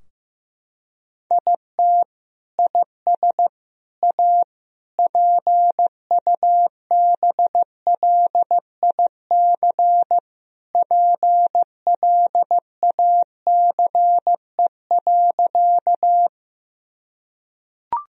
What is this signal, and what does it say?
Morse code, heard as a keyed tone, about 15 wpm, 700 Hz. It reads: IT IS A PUBLIC PLACE.
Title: Sentences from Top 1000 Words - Review Part 3 - 15wpm